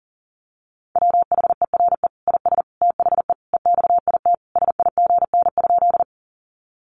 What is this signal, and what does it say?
Morse code, heard as a keyed tone, about 40 wpm, 700 Hz. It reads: WHERE IS THE EXIT SIGN?
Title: Sentences from Top 1100 Words 40wpm